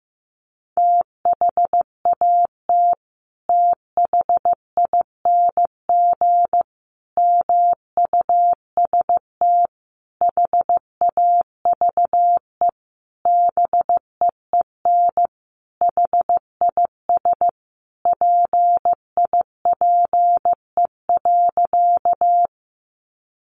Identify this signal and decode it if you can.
Morse code, heard as a keyed tone, about 15 wpm, 700 Hz. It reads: THAT THING MUST HAVE BEEN HIS PIPE.
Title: Sentences from Top 1500 Words 15wpm